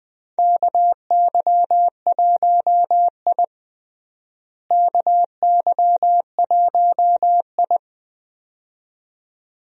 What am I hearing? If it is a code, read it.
Morse code, heard as a keyed tone, about 20 wpm, 700 Hz. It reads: KY1I KY1I
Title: USA Call Signs - 4 Characters - Repeated 2x - 20wpm